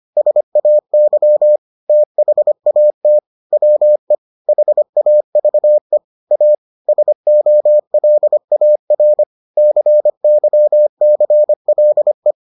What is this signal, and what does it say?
Morse code, heard as a keyed tone, about 25 wpm, 600 Hz. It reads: SAY THAT WE HAVE A SOLAR CYCLE